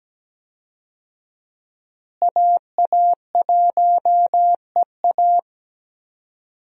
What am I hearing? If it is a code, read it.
Morse code, heard as a keyed tone, about 17 wpm, 700 Hz. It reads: AA1EA
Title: USA Call Signs - 5 Characters - Repeated 3x - 17wpm